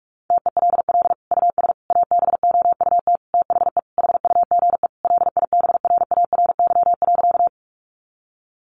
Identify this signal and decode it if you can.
Morse code, heard as a keyed tone, about 44 wpm, 700 Hz. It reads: TELL US ABOUT THE HUGE LIBRARY.
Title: Sentences from Top 1100 Words 45wpm